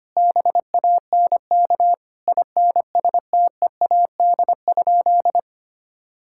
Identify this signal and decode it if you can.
Morse code, heard as a keyed tone, about 25 wpm, 700 Hz. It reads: BANK INSTEAD?
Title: Sentences from Top 1000 Words - Review Part 6 - 25wpm